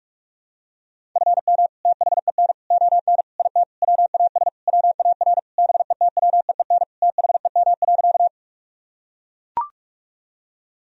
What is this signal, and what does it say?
Morse code, heard as a keyed tone, about 42 wpm, 700 Hz. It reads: FM THEN ON IT WAS WAR BETWEEN THEM.
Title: Sentences from Top 400 Words and Abbreviations 45wpm